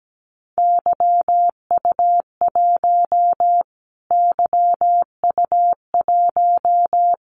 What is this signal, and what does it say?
Morse code, heard as a keyed tone, about 17 wpm, 700 Hz. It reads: YU1 YU1
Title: Callsign Prefixes for Top 25 DXCC - Repeated 3x - 17wpm